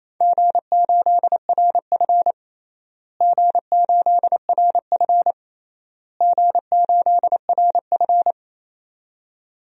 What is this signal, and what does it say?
Morse code, heard as a keyed tone, about 28 wpm, 700 Hz. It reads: G8RF G8RF G8RF
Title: DXCC-100 Call Signs - 4 Characters - Repeated 3x - 28wpm